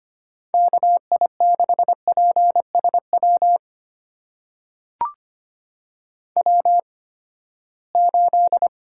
Morse code, heard as a keyed tone, about 25 wpm, spaced as for 17 wpm, 700 Hz. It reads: KI6PSW W8
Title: USA Call Signs - 8x Character Spacing - 25wpm